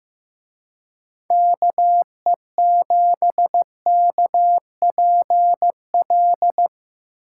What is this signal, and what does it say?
Morse code, heard as a keyed tone, about 17 wpm, 700 Hz. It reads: KE7KPL
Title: Speed Racing - USA Call Signs 15wpm